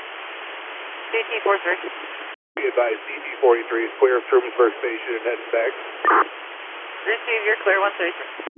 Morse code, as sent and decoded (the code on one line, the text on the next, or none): none